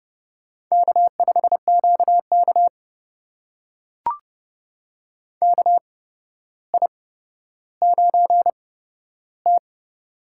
Morse code, heard as a keyed tone, about 30 wpm, spaced as for 13 wpm, 700 Hz. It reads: K5QK KI9T
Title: USA Call Signs - 4 Characters - 8x Character Spacing - 30wpm